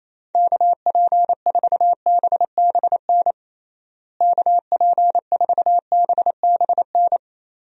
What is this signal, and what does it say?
Morse code, heard as a keyed tone, about 28 wpm, 700 Hz. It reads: KP4BBN KP4BBN
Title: DXCC-100 Call Signs - 6 Characters - Repeated 2x - 28wpm